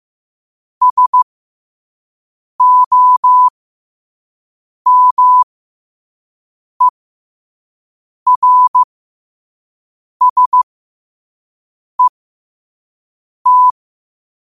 Morse code, heard as a keyed tone, about 15 wpm, spaced as for 5 wpm, 1000 Hz. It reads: SOMERSET
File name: qotdfiles / 5wpm